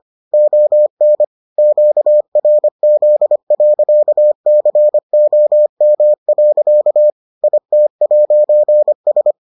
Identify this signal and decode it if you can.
Morse code, heard as a keyed tone, about 25 wpm, 600 Hz. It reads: ON QRZ.COM. IT'S